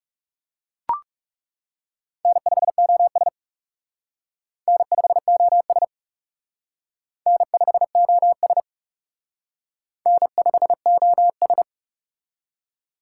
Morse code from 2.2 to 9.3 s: N5OS N5OS N5OS

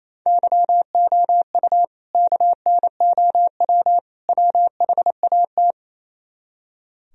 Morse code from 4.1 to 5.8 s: WHAT

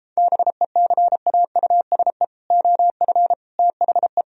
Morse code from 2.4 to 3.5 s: OF

